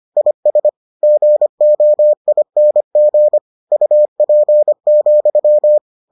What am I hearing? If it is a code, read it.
Morse code, heard as a keyed tone, about 25 wpm, 600 Hz. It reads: IS GOING UP,